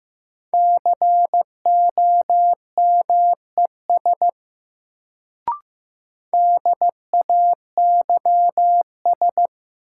Morse code, heard as a keyed tone, about 16 wpm, 700 Hz. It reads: COMES DAYS
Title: Sentences from Top 1000 Words - Review Part 5 - New Word Drill - 15wpm